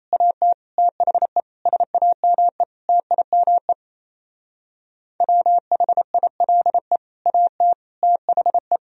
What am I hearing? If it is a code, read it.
Morse code, heard as a keyed tone, about 34 wpm, 700 Hz. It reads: AT THE SAME TIME WHILE AT THE
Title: Speed-Racing - Sets of 5 Words 28wpm